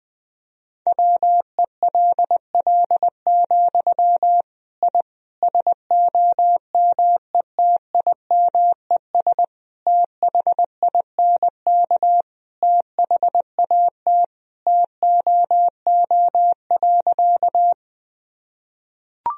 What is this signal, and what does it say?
Morse code, heard as a keyed tone, about 20 wpm, 700 Hz. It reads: WELL, I SOMETIMES THINK THAT TOO.
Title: Sentences from Top 1000 Words - Review Part 3 - 20wpm